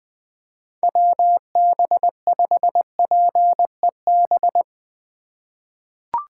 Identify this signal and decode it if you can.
Morse code, heard as a keyed tone, about 20 wpm, 700 Hz. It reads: WB5PEB
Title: USA Call Signs - 6 Characters - 3x Character Spacing - 20wpm